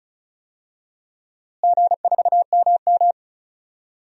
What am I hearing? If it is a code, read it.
Morse code, heard as a keyed tone, about 36 wpm, 700 Hz. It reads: G4MM
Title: DXCC-100 Call Signs - 4 Characters - Repeated 2x - 35wpm